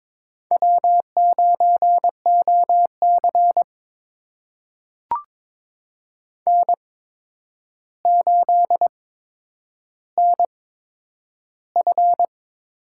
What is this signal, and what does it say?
Morse code, heard as a keyed tone, about 21 wpm, spaced as for 9 wpm, 700 Hz. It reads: W9OC N8NF
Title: USA Call Signs - 8x Character Spacing - 22wpm